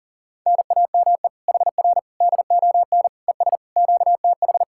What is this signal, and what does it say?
Morse code, heard as a keyed tone, about 47 wpm, 700 Hz. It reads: NAME HR DON ES QTH
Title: Speed-Racing Common QSO Phrases - 40wpm